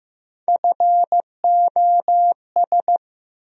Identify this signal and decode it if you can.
Morse code, heard as a keyed tone, about 15 wpm, 700 Hz. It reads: FOS
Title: USA Call Sign Postfixes - 3 Letters 15wpm